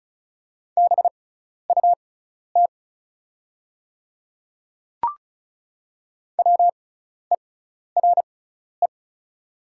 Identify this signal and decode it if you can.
Morse code, heard as a keyed tone, about 35 wpm, spaced as for 12 wpm, 690 Hz. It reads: BUT WERE
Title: 6x Character Spacing - Top 100 Words - 35wpm